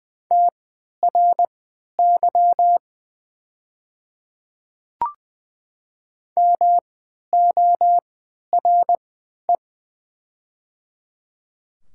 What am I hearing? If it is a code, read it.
Morse code, heard as a keyed tone, about 20 wpm, spaced as for 11 wpm, 700 Hz. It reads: TRY MORE
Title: 3x Character Spacing - Top 300 Words - 20wpm